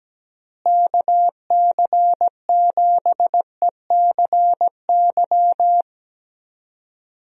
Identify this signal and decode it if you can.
Morse code, heard as a keyed tone, about 17 wpm, 700 Hz. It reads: KC7ECY
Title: USA Call Signs - 6 Characters - Repeated 3x - 17wpm